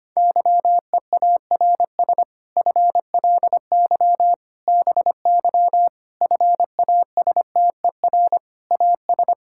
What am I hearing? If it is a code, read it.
Morse code, heard as a keyed tone, about 25 wpm, 700 Hz. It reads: YEARS FLY BY FASTER AS
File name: practice / 25wpm / Difficult Sentences with Hints - 25wpm